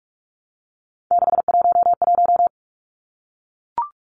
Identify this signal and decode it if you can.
Morse code, heard as a keyed tone, about 44 wpm, 700 Hz. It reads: 611
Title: Numbers - 3-Digit 45wpm